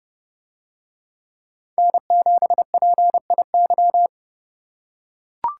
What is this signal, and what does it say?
Morse code, heard as a keyed tone, about 28 wpm, 700 Hz. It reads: N7PIY